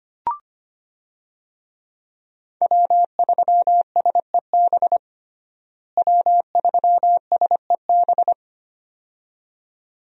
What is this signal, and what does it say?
Morse code, heard as a keyed tone, about 25 wpm, 700 Hz. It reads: W3SEB W3SEB
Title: USA Call Signs - 5 Characters - Repeated 2x - 25wpm